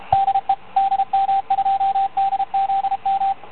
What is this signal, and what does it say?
Morse code, heard as a keyed tone, about 17 wpm, 780 Hz, with radio noise. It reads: DE DM2DZM